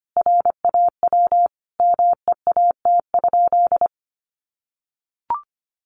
Morse code, heard as a keyed tone, about 25 wpm, 700 Hz. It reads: RAW MEAT?
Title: Sentences from Top 1500 Words 25wpm